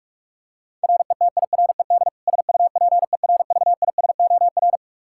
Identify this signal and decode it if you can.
Morse code, heard as a keyed tone, about 46 wpm, 700 Hz. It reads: RETIRED SUPERVISOR
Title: K1USN SST QSO Practice 45wpm